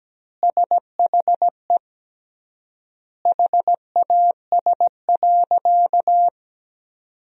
Morse code, heard as a keyed tone, about 17 wpm, 700 Hz. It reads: SHE HAS.
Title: 3x Word-Spacing - Sentences from Top 500 Words - 17wpm